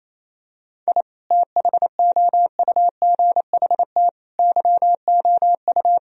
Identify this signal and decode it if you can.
Morse code, heard as a keyed tone, about 28 wpm, 700 Hz. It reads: I THOUGHT YOU
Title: Sentences from Top 1000 Words - Review Part 4 - 28wpm